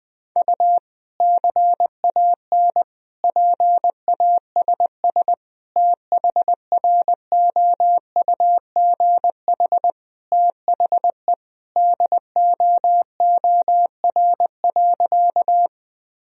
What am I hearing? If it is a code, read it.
Morse code, heard as a keyed tone, about 20 wpm, 700 Hz. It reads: U CAN PASS THROUGH THE DOOR.
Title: Sentences from Top 400 Words and Abbreviations 20wpm